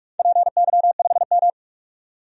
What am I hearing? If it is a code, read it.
Morse code, heard as a keyed tone, about 41 wpm, 700 Hz. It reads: PY5M
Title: DXCC-100 Call Signs - 4 Characters - 45wpm